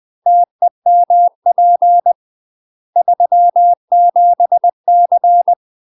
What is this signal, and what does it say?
Morse code, heard as a keyed tone, about 20 wpm, 700 Hz. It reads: TEMP 37C
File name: UK MASTER QSOs TRAINING / 20WPM / UK MASTER QSOs 0025